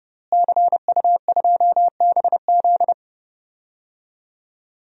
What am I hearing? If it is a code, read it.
Morse code, heard as a keyed tone, about 30 wpm, 700 Hz. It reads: CU2BZ
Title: DXCC-100 Call Signs - 5 Characters - Repeated 2x - 30wpm